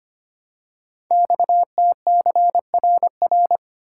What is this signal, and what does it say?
Morse code, heard as a keyed tone, about 25 wpm, 700 Hz. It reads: XTCRR
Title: Code Groups - 5 Letters - 25wpm